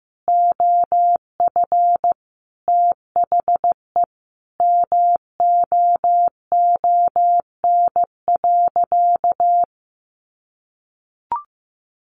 Morse code, heard as a keyed tone, about 15 wpm, 700 Hz. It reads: OF THE MOON.
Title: Sentences from Top 1500 Words 15wpm